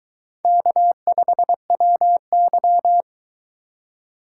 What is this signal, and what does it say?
Morse code, heard as a keyed tone, about 23 wpm, 700 Hz. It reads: K5WY